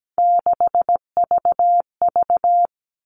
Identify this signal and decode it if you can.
Morse code, heard as a keyed tone, about 17 wpm, 700 Hz. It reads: 6VV
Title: All Cut Numbers - 3-Digit 17wpm